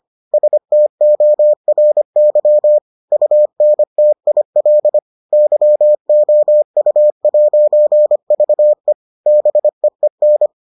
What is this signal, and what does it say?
Morse code, heard as a keyed tone, about 25 wpm, 600 Hz. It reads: STORY UNTIL YOU'VE BEEN